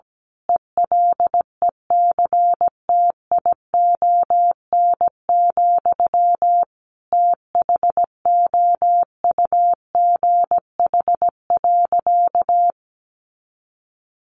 Morse code, heard as a keyed tone, about 17 wpm, 700 Hz. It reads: ELECTION, THOUGH.
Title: Sentences from Top 1500 Words 17wpm